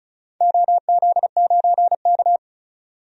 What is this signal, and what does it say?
Morse code, heard as a keyed tone, about 36 wpm, 690 Hz. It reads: OZ9K
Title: DXCC-100 Call Signs - 4 Characters - 35wpm